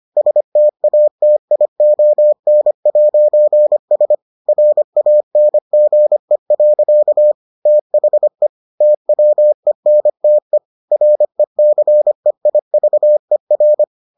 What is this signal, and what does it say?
Morse code, heard as a keyed tone, about 25 wpm, 600 Hz. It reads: STATION'S RANGE. THE TWENTE RECEIVER